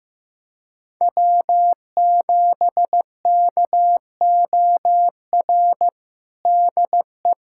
Morse code, heard as a keyed tone, about 15 wpm, 700 Hz. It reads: W7KOR DE
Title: USA Callsign DE Callsign 15wpm